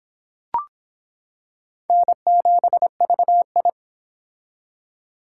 Morse code, heard as a keyed tone, about 26 wpm, 700 Hz. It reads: N7VI